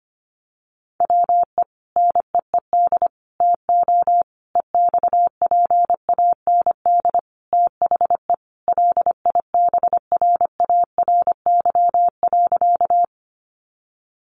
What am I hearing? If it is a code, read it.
Morse code, heard as a keyed tone, about 25 wpm, 700 Hz. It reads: WE NEED TO EXPAND THE LIBRARY.
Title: Sentences from Top 1400 Words 25wpm